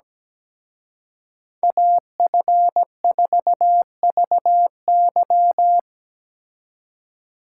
Morse code, heard as a keyed tone, about 20 wpm, 700 Hz. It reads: AF4VY
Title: Speed Racing - USA 2x2 Call Signs 17wpm